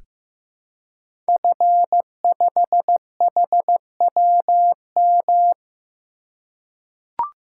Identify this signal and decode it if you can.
Morse code, heard as a keyed tone, about 15 wpm, 700 Hz. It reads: F5HWM
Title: Code Groups - 5 Letter:Numbers - 15wpm